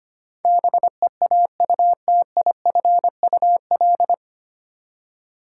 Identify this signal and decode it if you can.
Morse code, heard as a keyed tone, about 25 wpm, 700 Hz. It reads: BEAUTIFUL
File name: Nine Letter Words - 25wpm